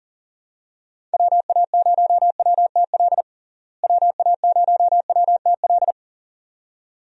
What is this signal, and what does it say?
Morse code, heard as a keyed tone, about 40 wpm, 700 Hz. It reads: WA0WTL WA0WTL
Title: USA Call Signs - 6 Characters - Repeated 2x - 40wpm